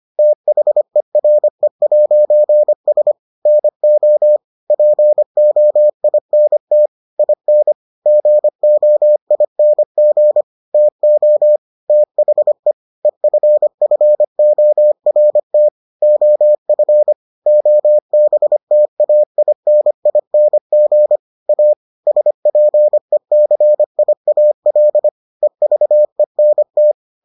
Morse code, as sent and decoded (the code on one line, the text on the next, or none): - .... .-. .----. ... / -. --- / .--. --- .. -. - / .. -. / --. --- .. -. --. / - --- / - .... / . ..-. ..-. --- .-. - / --- ..-. / --- -... - .- .. -. .. -. --. / .- / ... .--. -.-. .. .- .-.. / . ...- . -. -
THERE'S NO POINT IN GOING TO THE EFFORT OF OBTAINING A SPECIAL EVENT